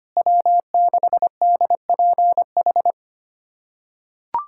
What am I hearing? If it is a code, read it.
Morse code, heard as a keyed tone, about 25 wpm, 700 Hz. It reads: W6DPH
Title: USA Callsign DE Callsign - 5 Characters - 25wpm